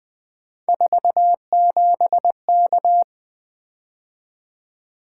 Morse code, heard as a keyed tone, about 20 wpm, 700 Hz. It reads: 47K
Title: Three Letter-Number Combinations - 20wpm